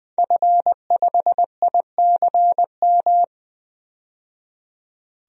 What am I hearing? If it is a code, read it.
Morse code, heard as a keyed tone, about 20 wpm, 700 Hz. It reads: F5ICM
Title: DXCC-100 Call Signs - 5 Characters - 20wpm